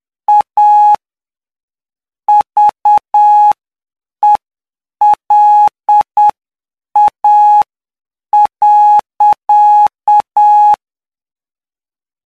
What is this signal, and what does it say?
Morse code, heard as a keyed tone, about 8 wpm, spaced as for 7 wpm, 840 Hz. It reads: A VELA.
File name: cw / py2lcd / 13-PY2LCD-Li��o 22